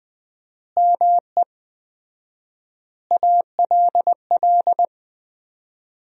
Morse code, heard as a keyed tone, about 20 wpm, 700 Hz. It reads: ME ALL
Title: 4x Word-Spacing - Sentences from Top 400 Words - 20wpm